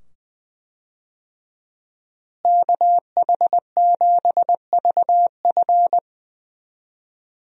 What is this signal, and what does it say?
Morse code, heard as a keyed tone, about 20 wpm, 700 Hz. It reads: KH7VF
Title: USA Call Signs - 5 Characters - Repeated 2x - 20wpm